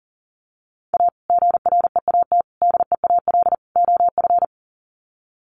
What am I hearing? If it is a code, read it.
Morse code, heard as a keyed tone, about 40 wpm, 700 Hz. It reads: A GREAT DEAL OF
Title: Sets of 4 Words 40wpm